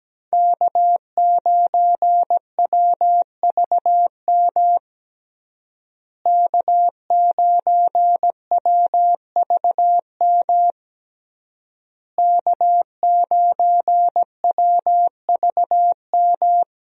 Morse code, heard as a keyed tone, about 17 wpm, 700 Hz. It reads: K9WVM K9WVM K9WVM
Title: USA Call Signs - 5 Characters - Repeated 3x - 17wpm